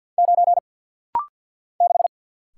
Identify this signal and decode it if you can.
Morse code, heard as a keyed tone, about 49 wpm, 700 Hz. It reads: 9 6
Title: Single Letter-Number - Rapid-Fire 50wpm